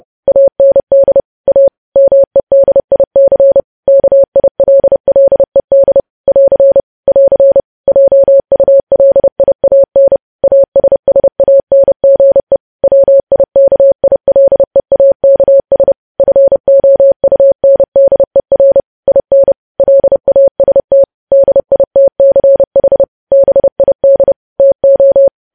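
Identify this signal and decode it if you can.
Morse code, heard as a keyed tone, about 30 wpm, 550 Hz. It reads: AND A MEDIC KILLED <AR> <AR> JULIAN ASSANGE WIKILEAKS FOUNDER IN LAST DITCH BID TO
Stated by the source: Morse code news headlines sent at 30 WPM.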